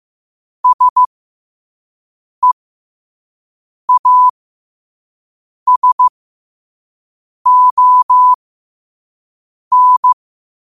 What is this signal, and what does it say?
Morse code, heard as a keyed tone, about 15 wpm, spaced as for 5 wpm, 1000 Hz. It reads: SEASON
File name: qotdfiles / 5wpm